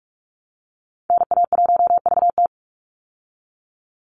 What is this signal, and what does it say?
Morse code, heard as a keyed tone, about 44 wpm, 700 Hz. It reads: NA1VT
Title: Call Signs Repeated 3x - 45wpm